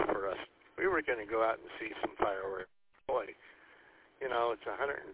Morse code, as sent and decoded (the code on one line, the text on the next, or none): none